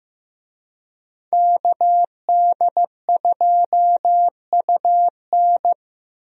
Morse code, heard as a keyed tone, about 15 wpm, 700 Hz. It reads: KD2UN
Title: USA Call Signs - 5 Characters - Repeated 3x - 15wpm